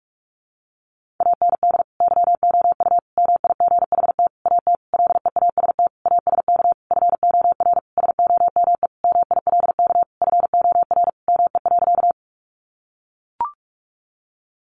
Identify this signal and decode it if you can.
Morse code, heard as a keyed tone, about 44 wpm, 700 Hz. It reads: AND YOU MIGHT AT LEAST ASK FOR SOME MILK FOR ME.
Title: Sentences from Top 1500 Words 45wpm